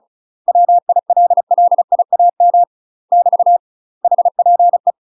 Morse code, heard as a keyed tone, about 34 wpm, 700 Hz. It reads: WILLIAM <BT> HPE
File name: UK MASTER QSOs TRAINING / 35WPM / UK MASTER QSOs 0015